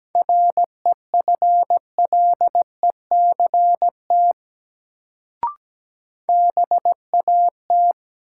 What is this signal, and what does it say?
Morse code, heard as a keyed tone, about 17 wpm, 700 Hz. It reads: REFLECT BAT
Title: Sentences from Top 1500 Words - New Word Drill - 17wpm